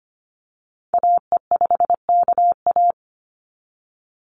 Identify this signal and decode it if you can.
Morse code, heard as a keyed tone, about 25 wpm, 690 Hz. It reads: AE5KA
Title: NA SOTA Activator Callsigns 25wpm